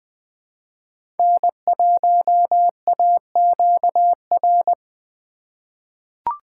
Morse code, heard as a keyed tone, about 20 wpm, 700 Hz. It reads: N1AQR
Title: USA Call Signs - 5 Characters - 4x Character Spacing - 20wpm